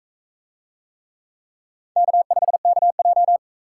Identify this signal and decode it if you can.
Morse code, heard as a keyed tone, about 39 wpm, 700 Hz. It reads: K5KJ